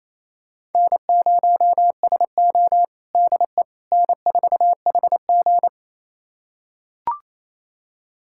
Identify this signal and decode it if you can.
Morse code, heard as a keyed tone, about 28 wpm, 700 Hz. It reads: N0SO DE N4HG
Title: USA Callsign DE Callsign - 4 Characters - 28wpm